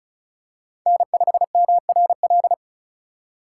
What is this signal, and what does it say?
Morse code, heard as a keyed tone, about 36 wpm, 700 Hz. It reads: N5MRL
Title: USA Call Signs - 5 Characters - 4x Character Spacing - 35wpm